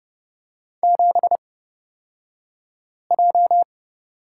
Morse code, heard as a keyed tone, about 30 wpm, spaced as for 5 wpm, 700 Hz. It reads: 7J
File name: Lesson 36d - BT - Two Characters 30wpm